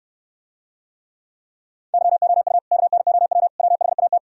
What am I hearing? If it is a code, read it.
Morse code, heard as a keyed tone, about 67 wpm, 700 Hz. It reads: YOU KNOW WHAT